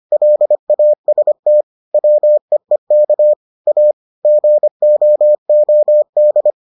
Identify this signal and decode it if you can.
Morse code, heard as a keyed tone, about 25 wpm, 600 Hz. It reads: LAST WEEK A GOOD